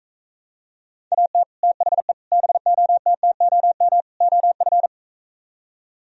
Morse code, heard as a keyed tone, about 45 wpm, 700 Hz. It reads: AT THE BOTTOM OF